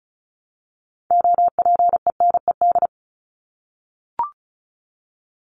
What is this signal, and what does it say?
Morse code, heard as a keyed tone, about 34 wpm, 700 Hz. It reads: OPENED
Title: ED Suffix - Top 100 Words with Base Words 35wpm